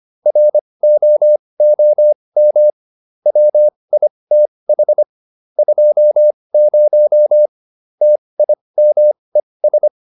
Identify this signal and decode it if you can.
Morse code, heard as a keyed tone, about 25 wpm, spaced as for 20 wpm, 600 Hz. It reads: ROOM WITH 20 TIMES